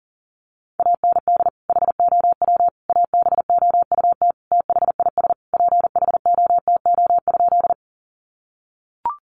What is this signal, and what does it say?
Morse code, heard as a keyed tone, about 40 wpm, 700 Hz. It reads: AND HOW ABOUT THIS PHOTO?
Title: Sentences from Top 1200 Words 40wpm